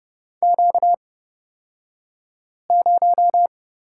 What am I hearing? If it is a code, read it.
Morse code, heard as a keyed tone, about 30 wpm, spaced as for 5 wpm, 700 Hz. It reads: Q0
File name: Lesson 38d - Letter q - Two Characters 30wpm